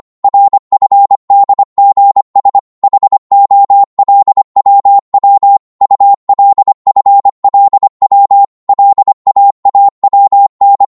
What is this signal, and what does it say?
Morse code, heard as a keyed tone, about 25 wpm, 800 Hz. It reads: RFDGS HOLWW ULFLW LAAWN
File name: Morse / ltr-25wpm